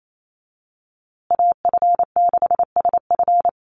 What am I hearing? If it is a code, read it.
Morse code, heard as a keyed tone, about 28 wpm, 700 Hz. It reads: AF6SF
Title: USA Call Signs Repeated 3x 28wpm